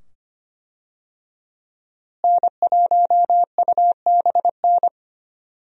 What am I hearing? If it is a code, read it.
Morse code, heard as a keyed tone, about 25 wpm, 700 Hz. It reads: N1UBN
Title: USA Call Signs - 5 Characters - Repeated 2x - 25wpm